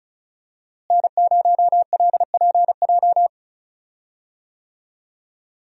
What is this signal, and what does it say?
Morse code, heard as a keyed tone, about 34 wpm, 700 Hz. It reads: N0LPJ
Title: USA Call Signs - 5 Characters - Repeated 2x - 35wpm